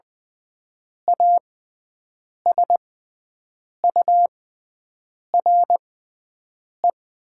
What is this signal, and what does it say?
Morse code, heard as a keyed tone, about 20 wpm, spaced as for 7 wpm, 700 Hz. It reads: ASURE
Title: 6x Character Spacing - Top 500 Words - 20wpm